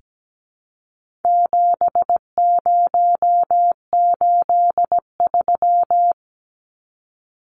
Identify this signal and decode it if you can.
Morse code, heard as a keyed tone, about 17 wpm, 700 Hz. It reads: 7083
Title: Numbers - 4-Digit 17wpm